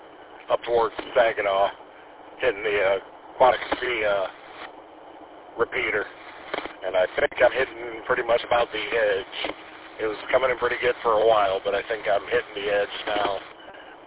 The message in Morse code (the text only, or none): none